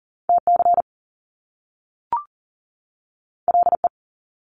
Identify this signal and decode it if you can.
Morse code, heard as a keyed tone, about 40 wpm, 700 Hz. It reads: TC LE
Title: Top Two Letter Combinations in Words 40wpm